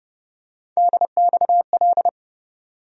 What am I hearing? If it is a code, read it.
Morse code, heard as a keyed tone, about 30 wpm, 700 Hz. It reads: DXL
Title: USA Call Sign Postfixes - 3 Letters 30wpm